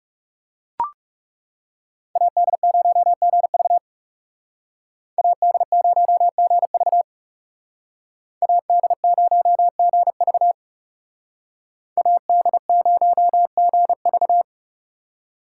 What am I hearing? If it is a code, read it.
Morse code, heard as a keyed tone, about 35 wpm, 700 Hz. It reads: AD0GV AD0GV AD0GV AD0GV
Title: Speed Racing - USA 2x2 Call Signs 30wpm